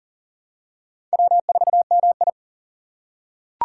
Morse code, heard as a keyed tone, about 37 wpm, 700 Hz. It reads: W4MI
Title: USA Call Signs - 2x Character Spacing - 40wpm